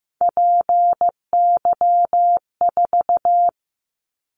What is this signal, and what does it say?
Morse code, heard as a keyed tone, about 15 wpm, 700 Hz. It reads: PY4
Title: Callsign Prefixes for Top 100 DXCC - 15wpm